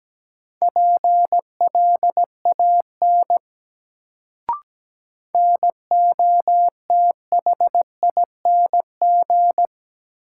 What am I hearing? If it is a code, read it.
Morse code, heard as a keyed tone, about 17 wpm, 700 Hz. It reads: PLAN NOTHING
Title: Sentences from Top 500 Words - New Word Drill - 17wpm